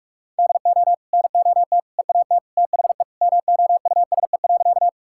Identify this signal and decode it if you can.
Morse code, heard as a keyed tone, about 46 wpm, 700 Hz. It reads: DO NOT EAT THE MOUSE.
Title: Sentences from Top 1000 Words - Review Part 5 - 45wpm